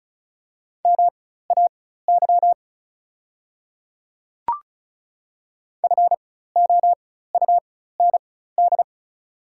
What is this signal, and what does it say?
Morse code, heard as a keyed tone, about 36 wpm, spaced as for 16 wpm, 690 Hz. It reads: MAY FOUND
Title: 4x Character Spacing - Top 300 Words - 35wpm